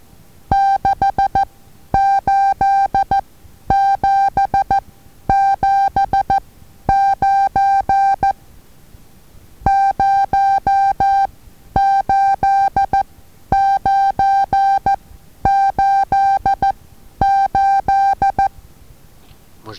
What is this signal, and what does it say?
Morse code, heard as a keyed tone, about 14 wpm, spaced as for 10 wpm, 790 Hz. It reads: 68779 08988
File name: Taviro HA6VA A